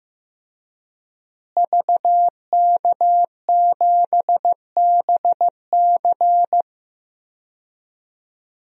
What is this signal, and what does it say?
Morse code, heard as a keyed tone, about 15 wpm, 700 Hz. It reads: VK7BC
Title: DXCC-100 Call Signs - 5 Characters - Repeated 3x - 15wpm